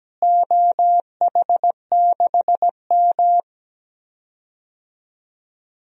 Morse code, heard as a keyed tone, about 17 wpm, 700 Hz. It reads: OH6M
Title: DXCC-100 Call Signs - 4 Characters - Repeated 3x - 17wpm